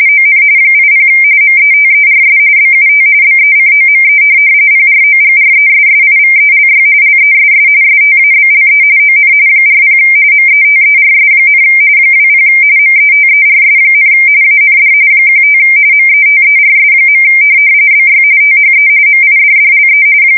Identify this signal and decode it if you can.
RTTY, about 45.45 baud. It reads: ON 80 METERS BUT BECOME PROGRESSIVELY LESS STABLE ON 40 METERS, 20 METERS, AND 15 METERS. BY THE MIDDLE
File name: fsk170 / teletype-RTTY